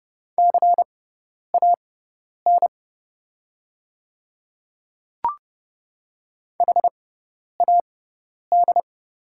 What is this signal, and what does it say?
Morse code, heard as a keyed tone, about 30 wpm, spaced as for 10 wpm, 690 Hz. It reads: CAN HAD
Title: 6x Character Spacing - Top 100 Words - 30wpm